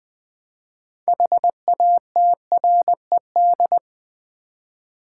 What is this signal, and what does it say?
Morse code, heard as a keyed tone, about 20 wpm, 700 Hz. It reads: HATRED
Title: Build A Word - 2 Letters At A Time - 6-Letter Words - 20wpm